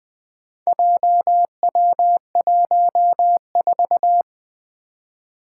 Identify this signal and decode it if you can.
Morse code, heard as a keyed tone, about 20 wpm, 700 Hz. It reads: JW14
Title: Four Challenging Letter-Number Combinations 20wpm